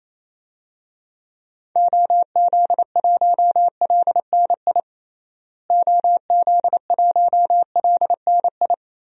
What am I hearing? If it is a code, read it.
Morse code, heard as a keyed tone, about 27 wpm, 700 Hz. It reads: OZ1LNI OZ1LNI
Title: DXCC-100 Call Signs - 6 Characters - Repeated 3x - 28wpm